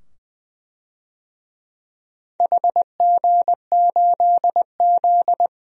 Morse code, heard as a keyed tone, about 20 wpm, 700 Hz. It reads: HG8Z
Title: DXCC-100 Call Signs - 4 Characters - 20wpm